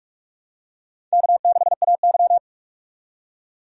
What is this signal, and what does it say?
Morse code, heard as a keyed tone, about 42 wpm, 700 Hz. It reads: K6AY
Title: USA Call Signs - 4 Characters - 45wpm